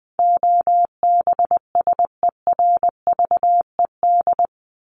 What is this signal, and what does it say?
Morse code, heard as a keyed tone, about 20 wpm, 700 Hz. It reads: OBSERVED
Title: ED Suffix - Top 100 Words with Base Words 20wpm